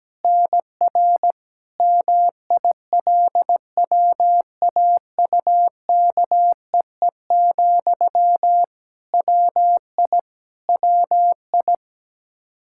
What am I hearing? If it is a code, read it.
Morse code, heard as a keyed tone, about 17 wpm, 700 Hz. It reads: NR MILWAUKEE, WI WI
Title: Practice QSO - Exchange 1 - Beginning - 17wpm